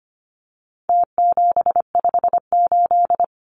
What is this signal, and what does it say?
Morse code, heard as a keyed tone, about 25 wpm, 700 Hz. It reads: T758
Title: Numbers - Contest Style - 4-Digit 25wpm